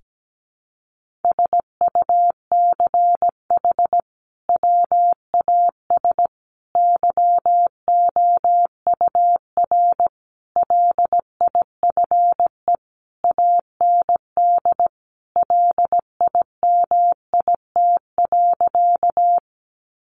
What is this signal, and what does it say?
Morse code, heard as a keyed tone, about 17 wpm, 700 Hz. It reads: SUCH WAS YOUR LIFE AND LIMIT.
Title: Sentences from Top 1500 Words 17wpm